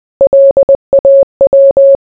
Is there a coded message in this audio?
Yes: Morse code from 0.2 to 2.0 s, heard as a keyed tone, about 20 wpm, 550 Hz. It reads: LAW